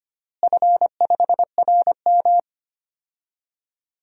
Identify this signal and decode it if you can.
Morse code, heard as a keyed tone, about 25 wpm, 700 Hz. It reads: F5RM
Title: DXCC-100 Call Signs - 4 Characters - Repeated 3x - 25wpm